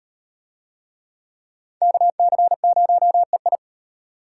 Morse code, heard as a keyed tone, about 32 wpm, 700 Hz. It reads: KC0EI